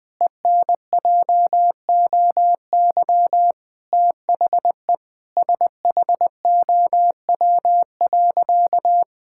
Morse code, heard as a keyed tone, about 20 wpm, 700 Hz. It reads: ENJOY THE SHOW.
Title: Sentences with 4-Word Phrases - 20wpm